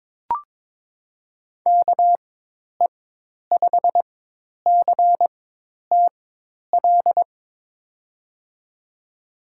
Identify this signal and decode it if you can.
Morse code, heard as a keyed tone, about 22 wpm, spaced as for 10 wpm, 700 Hz. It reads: KE5CTL
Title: USA Call Signs - 6 Characters - 4x Character Spacing - 22wpm